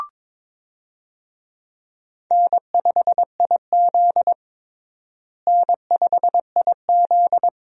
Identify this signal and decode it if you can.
Morse code, heard as a keyed tone, about 22 wpm, 700 Hz. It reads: N5IZ N5IZ
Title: USA Call Signs - 4 Characters - Repeated 2x - 22wpm